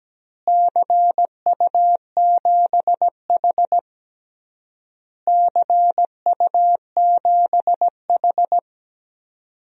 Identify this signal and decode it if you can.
Morse code, heard as a keyed tone, about 17 wpm, 700 Hz. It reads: CU7H CU7H
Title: DXCC-100 Call Signs - 4 Characters - Repeated 3x - 17wpm